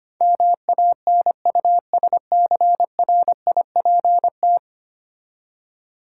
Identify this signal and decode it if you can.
Morse code, heard as a keyed tone, about 25 wpm, 700 Hz. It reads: MANUSCRIPT
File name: Ten Letter Words - 25wpm